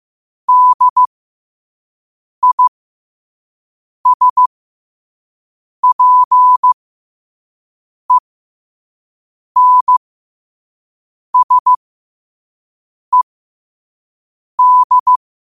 Morse code, heard as a keyed tone, about 15 wpm, spaced as for 5 wpm, 1000 Hz. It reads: DISPENSED
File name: qotdfiles / 5wpm